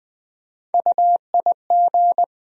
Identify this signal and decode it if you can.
Morse code, heard as a keyed tone, about 20 wpm, 700 Hz. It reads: UIG
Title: USA Call Sign Postfixes - 3 Letters 20wpm